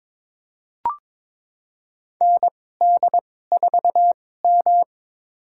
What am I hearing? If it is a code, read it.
Morse code, heard as a keyed tone, about 22 wpm, spaced as for 16 wpm, 700 Hz. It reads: ND4M
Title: USA Call Signs - 4 Characters - 2x Character Spacing - 22wpm